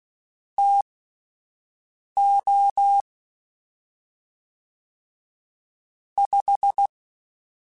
Morse code, heard as a keyed tone, about 16 wpm, spaced as for 5 wpm, 800 Hz. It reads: TO 5
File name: Spaced Characters / Spaced Characters-mp3s / Weather4 21 sp @10